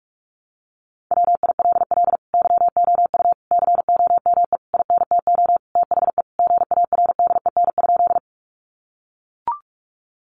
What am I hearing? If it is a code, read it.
Morse code, heard as a keyed tone, about 44 wpm, 700 Hz. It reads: WILL YOU COME INTO THE GARDEN?
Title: Sentences from Top 700 Words 45wpm